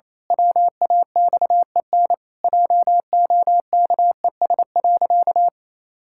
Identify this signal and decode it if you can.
Morse code, heard as a keyed tone, about 28 wpm, 700 Hz. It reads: WAXEN JOKES.
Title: Pangram Challenge v2 - 28wpm